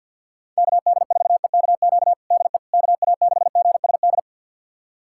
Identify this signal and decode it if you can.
Morse code, heard as a keyed tone, about 53 wpm, 700 Hz. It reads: KD4EKQ DE KA6GSD
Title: USA Callsign DE Callsign - 6 Characters - 50wpm